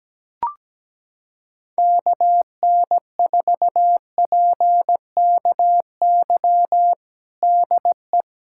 Morse code, heard as a keyed tone, about 17 wpm, 700 Hz. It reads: KN4PKY DE
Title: USA Callsign DE Callsign - 6 Characters - 17wpm